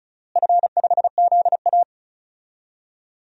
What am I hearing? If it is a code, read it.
Morse code, heard as a keyed tone, about 36 wpm, 700 Hz. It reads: F5ZA
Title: DXCC-100 Call Signs - 4 Characters - Repeated 2x - 35wpm